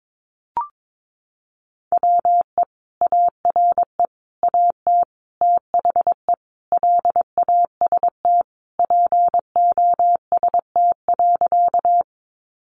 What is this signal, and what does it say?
Morse code, heard as a keyed tone, about 22 wpm, 700 Hz. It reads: WE ARE AT THE LAST POST.
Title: Sentences from Top 1000 Words 22wpm